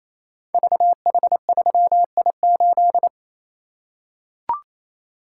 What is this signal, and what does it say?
Morse code, heard as a keyed tone, about 27 wpm, 700 Hz. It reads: VH3I8
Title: Code Groups - 5 Letter:Numbers - 28wpm